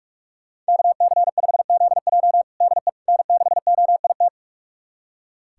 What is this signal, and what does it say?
Morse code, heard as a keyed tone, about 46 wpm, 700 Hz. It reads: KC5ZJ DE N6OIT
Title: USA Callsign DE Callsign - 5 Characters - 45wpm